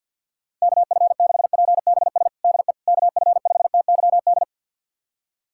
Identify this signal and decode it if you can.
Morse code, heard as a keyed tone, about 53 wpm, 700 Hz. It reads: KF6PBS DE KF5TYD
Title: USA Callsign DE Callsign - 6 Characters - 50wpm